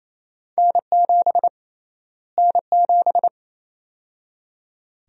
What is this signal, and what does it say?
Morse code, heard as a keyed tone, about 27 wpm, 700 Hz. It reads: N7 N7
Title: USA Call Sign Prefixes - Repeated 2x - 28wpm